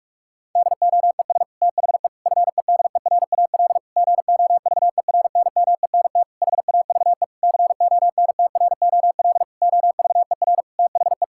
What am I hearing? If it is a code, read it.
Morse code, heard as a keyed tone, about 46 wpm, 700 Hz. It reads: DOES THE FEDERAL GOVERNMENT HAVE CONTROL OVER THE
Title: Sentences from Top 1000 Words - Review Part 5 - 45wpm